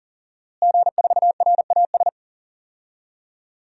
Morse code, heard as a keyed tone, about 40 wpm, 700 Hz. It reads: G4RAS
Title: DXCC-100 Call Signs - 5 Characters - Repeated 2x - 40wpm